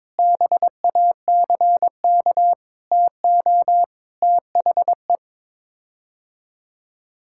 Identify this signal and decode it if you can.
Morse code, heard as a keyed tone, about 22 wpm, 700 Hz. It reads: BACK TO THE
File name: practice / 22wpm / Sets of 6 Words - 22wpm